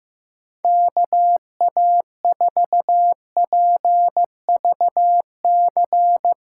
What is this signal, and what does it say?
Morse code, heard as a keyed tone, about 15 wpm, 700 Hz. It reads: KA4PVC
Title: USA Call Signs - 8x Character Spacing - 15wpm